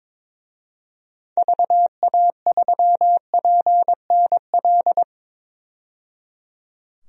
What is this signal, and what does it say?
Morse code, heard as a keyed tone, about 22 wpm, 700 Hz. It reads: VA3PNL
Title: DXCC-100 Call Signs - 6 Characters - 22wpm